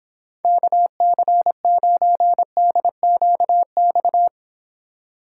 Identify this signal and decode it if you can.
Morse code, heard as a keyed tone, about 29 wpm, 700 Hz. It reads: KC9DQX